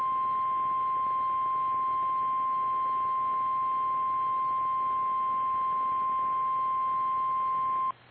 Emergency Alert System (SAME) tones (the 1050 Hz attention tone). It is heard from 0.3 to 7.5 s.
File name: eas / KKPRKXPN / KKPRKXPN